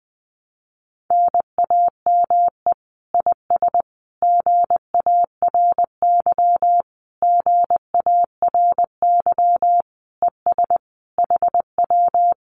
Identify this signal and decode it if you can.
Morse code, heard as a keyed tone, about 20 wpm, 700 Hz. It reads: NAME IS GARY GARY ES HW
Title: Common QSO Phrases Encore 20wpm